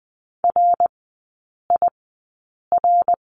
Morse code, heard as a keyed tone, about 20 wpm, spaced as for 8 wpm, 700 Hz. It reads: RIR
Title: 074 Lesson 13e - Letter L - Three Characters 20wpm